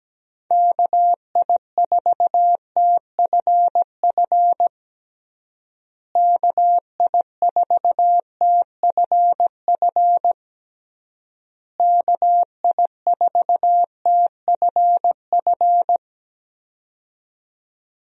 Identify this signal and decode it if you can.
Morse code, heard as a keyed tone, about 17 wpm, 700 Hz. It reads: KI4TFF KI4TFF KI4TFF
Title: USA Call Signs - 6 Characters - Repeated 3x - 17wpm